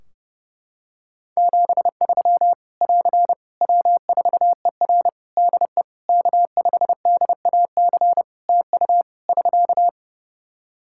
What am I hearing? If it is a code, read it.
Morse code, heard as a keyed tone, about 30 wpm, 700 Hz. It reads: 73 <AR> W4ER DE K5DAC TU <SK>
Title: K1USN SST QSO Practice 30wpm